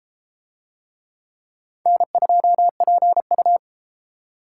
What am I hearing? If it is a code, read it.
Morse code, heard as a keyed tone, about 30 wpm, 700 Hz. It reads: N2PU